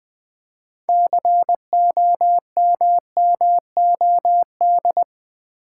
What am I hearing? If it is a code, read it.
Morse code, heard as a keyed tone, about 20 wpm, 700 Hz. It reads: COMMOD
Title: Build A Word - 3 Letters At A Time - 9-Letter Words - 20wpm